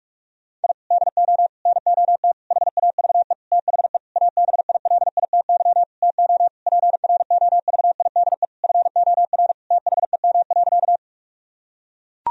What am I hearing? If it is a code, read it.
Morse code, heard as a keyed tone, about 41 wpm, 700 Hz. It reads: I DO NOT HAVE THE ABILITY TO PROVIDE FOR THEM.
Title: Sentences from Top 1000 Words - Review Part 3 - 45wpm